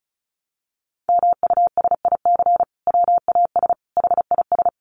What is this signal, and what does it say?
Morse code, heard as a keyed tone, about 34 wpm, 700 Hz. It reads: MUSIC WAS HIS
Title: Sentences from Top 1300 Words 35wpm